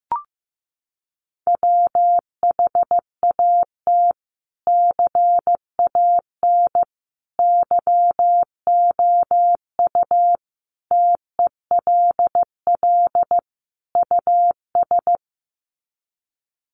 Morse code, heard as a keyed tone, about 15 wpm, 700 Hz. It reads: WHAT CAN YOU TELL US
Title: Sets of 5 Words Encore 15wpm